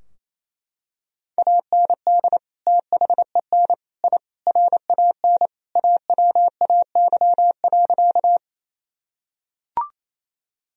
Morse code, heard as a keyed tone, about 28 wpm, 700 Hz. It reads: AND THEN I RAN AWAY.
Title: Sentences from Top 1000 Words - Review Part 3 - 28wpm